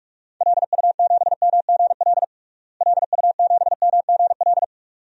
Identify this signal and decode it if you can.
Morse code, heard as a keyed tone, about 42 wpm, 700 Hz. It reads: LU7MGL LU7MGL
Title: DXCC-100 Call Signs - 6 Characters - Repeated 2x - 45wpm